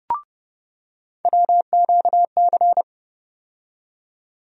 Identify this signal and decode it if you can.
Morse code, heard as a keyed tone, about 30 wpm, 700 Hz. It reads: WQC
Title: USA Call Sign Postfixes - 3 Letters 30wpm